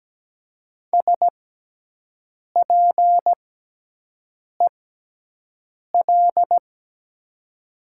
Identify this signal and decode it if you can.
Morse code, heard as a keyed tone, about 17 wpm, spaced as for 6 wpm, 700 Hz. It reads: SPEL
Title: 6x Character Spacing - Top 200 Words - 17wpm